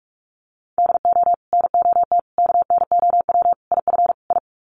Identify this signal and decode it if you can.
Morse code, heard as a keyed tone, about 44 wpm, 700 Hz. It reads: DO NOT KNOW IF I
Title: Sets of 5 Words 45wpm